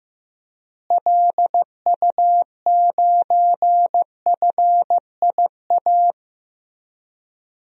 Morse code, heard as a keyed tone, about 16 wpm, 700 Hz. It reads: LU9FIA
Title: DXCC-100 Call Signs - 6 Characters - Repeated 3x - 15wpm